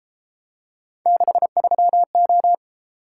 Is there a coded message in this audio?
Yes: Morse code from 1.1 to 2.5 s, heard as a keyed tone, about 31 wpm, 700 Hz. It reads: 63O